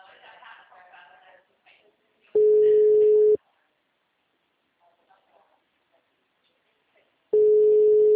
Ringback tone (Europe, 425 Hz).
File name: out- -2012- - - .20177